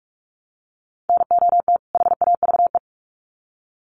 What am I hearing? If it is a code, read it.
Morse code, heard as a keyed tone, about 44 wpm, 700 Hz. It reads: NOT HAVE
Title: Sets of 2 Words 45wpm